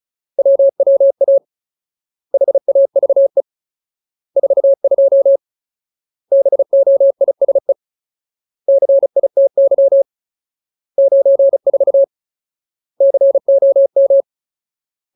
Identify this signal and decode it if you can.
Morse code, heard as a keyed tone, about 35 wpm, 550 Hz. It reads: WWA HAVE 42 BOISE CITY 94 COM
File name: Ultimate Word List / 35WPM Ultimate